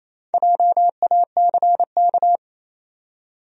Morse code, heard as a keyed tone, about 28 wpm, 700 Hz. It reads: JACK
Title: USA Common Names - 28wpm